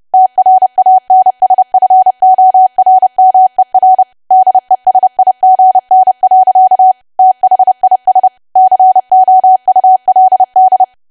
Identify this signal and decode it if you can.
Morse code, heard as a keyed tone, about 30 wpm, 750 Hz. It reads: TRANSFORMER DESIGN. THIS COULD
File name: cwa / W1AW-Level3 / 30WPM